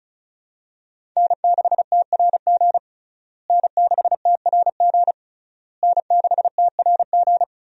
Morse code, heard as a keyed tone, about 34 wpm, 700 Hz. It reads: N6TRG N6TRG N6TRG
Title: USA Call Signs - 5 Characters - Repeated 3x - 35wpm